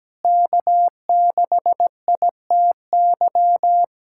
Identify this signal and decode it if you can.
Morse code, heard as a keyed tone, about 17 wpm, 700 Hz. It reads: K6ITY
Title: USA Callsign DE Callsign - 5 Characters - 17wpm